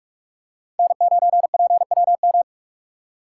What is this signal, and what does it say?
Morse code, heard as a keyed tone, about 46 wpm, 700 Hz. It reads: N9PWM
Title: USA Call Signs - 5 Characters - 2x Character Spacing - 45wpm